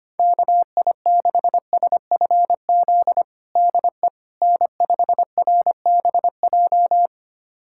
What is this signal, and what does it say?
Morse code, heard as a keyed tone, about 25 wpm, 700 Hz. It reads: KI6SFZ DE N5RBJ
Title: USA Callsign DE Callsign 25wpm